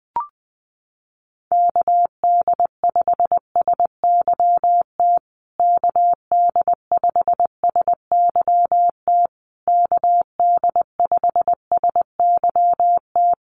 Morse code, heard as a keyed tone, about 20 wpm, 700 Hz. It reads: KD5SYT KD5SYT KD5SYT
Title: USA Call Signs Repeated 3x 20wpm